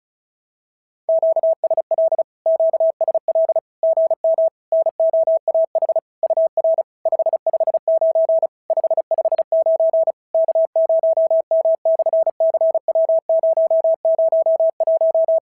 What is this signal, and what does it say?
Morse code, heard as a keyed tone, about 36 wpm, 650 Hz. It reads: QSL QSL GM NOAH UR 559 559 K0M/CW001
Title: SOTA QSO Practice 35wpm